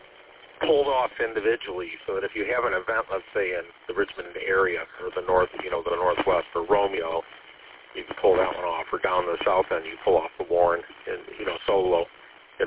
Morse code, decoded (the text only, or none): none